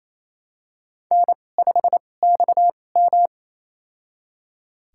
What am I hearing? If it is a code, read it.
Morse code, heard as a keyed tone, about 28 wpm, spaced as for 20 wpm, 700 Hz. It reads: N5XM
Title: USA Call Signs - 4 Characters - 2x Character Spacing - 28wpm